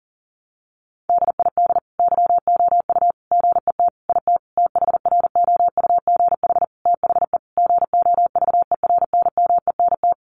Morse code, heard as a keyed tone, about 37 wpm, 700 Hz. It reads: DID YOU GET IT THROUGH THE GOVERNMENT
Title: Sentences from Top 1300 Words 40wpm